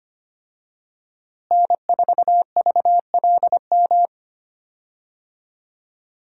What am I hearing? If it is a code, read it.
Morse code, heard as a keyed tone, about 25 wpm, 700 Hz. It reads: N4VLM
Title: USA Call Signs - 5 Characters - Repeated 2x - 25wpm